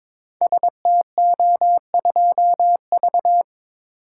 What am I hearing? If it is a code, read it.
Morse code, heard as a keyed tone, about 22 wpm, 690 Hz. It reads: STO2V
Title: Code Groups - 5 Letter:Numbers - 22wpm